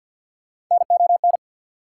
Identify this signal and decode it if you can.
Morse code, heard as a keyed tone, about 53 wpm, 690 Hz. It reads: NON
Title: Prefixes and Suffixes 50wpm